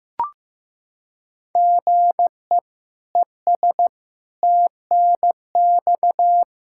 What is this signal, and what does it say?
Morse code, heard as a keyed tone, about 15 wpm, 700 Hz. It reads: GE ES TNX
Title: K1USN SST QSO Practice 15wpm